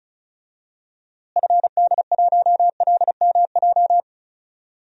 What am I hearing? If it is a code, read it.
Morse code, heard as a keyed tone, about 34 wpm, 700 Hz. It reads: FD1LMJ
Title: DXCC-100 Call Signs - 6 Characters - Repeated 2x - 35wpm